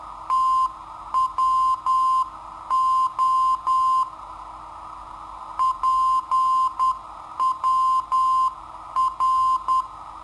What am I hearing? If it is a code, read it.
Morse code, heard as a keyed tone, about 10 wpm, 1040 Hz. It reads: TWO PWR
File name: sitoariold / beacon hk